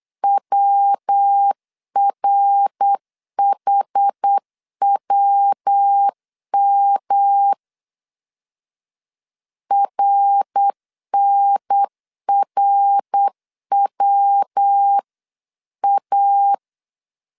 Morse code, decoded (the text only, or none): WRHWM RNRWA